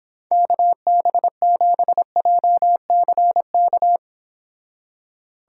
Morse code, heard as a keyed tone, about 26 wpm, 700 Hz. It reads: KB7JCK